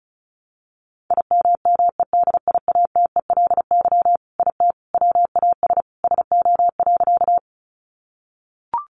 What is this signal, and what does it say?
Morse code, heard as a keyed tone, about 34 wpm, 700 Hz. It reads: IMMEDIATELY IT WAS SO.
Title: Sentences from Top 1100 Words 35wpm